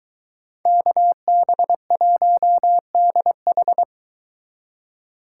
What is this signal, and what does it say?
Morse code, heard as a keyed tone, about 23 wpm, 700 Hz. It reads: KB1DH